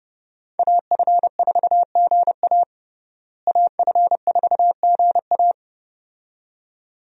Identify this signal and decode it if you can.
Morse code, heard as a keyed tone, about 30 wpm, 700 Hz. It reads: AF4GA AF4GA
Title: USA Call Signs - 5 Characters - Repeated 2x - 30wpm